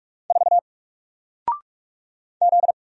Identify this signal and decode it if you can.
Morse code, heard as a keyed tone, about 46 wpm, 700 Hz. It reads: 4 Z
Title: Single Letter-Number 45wpm